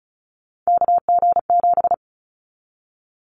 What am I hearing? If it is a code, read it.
Morse code, heard as a keyed tone, about 36 wpm, 700 Hz. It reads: KG7
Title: Callsign Prefixes for Top 25 DXCC - 35wpm